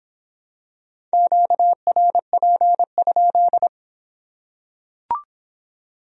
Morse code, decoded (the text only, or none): QRP?